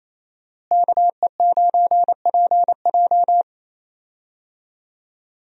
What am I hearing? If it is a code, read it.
Morse code, heard as a keyed tone, about 27 wpm, 700 Hz. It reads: KE9PJ
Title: USA Call Signs - 5 Characters - Repeated 2x - 28wpm